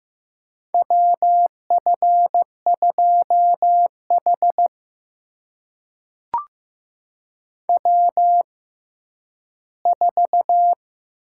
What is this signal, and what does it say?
Morse code, heard as a keyed tone, about 15 wpm, spaced as for 10 wpm, 700 Hz. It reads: WF2H W4
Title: USA Call Signs - 4 Characters - 6x Character Spacing - 15wpm